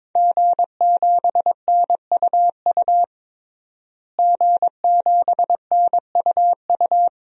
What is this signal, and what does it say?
Morse code, heard as a keyed tone, about 21 wpm, 700 Hz. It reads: G7NUU G7NUU
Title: DXCC-100 Call Signs - 5 Characters - Repeated 2x - 22wpm